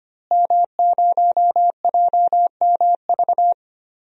Morse code, heard as a keyed tone, about 25 wpm, 700 Hz. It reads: M0JMV
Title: DXCC-100 Call Signs - 5 Characters - Repeated 3x - 25wpm